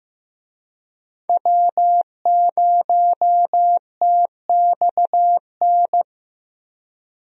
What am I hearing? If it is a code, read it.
Morse code, heard as a keyed tone, about 15 wpm, 700 Hz. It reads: W0TXN
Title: USA Call Signs - 5 Characters - Repeated 3x - 15wpm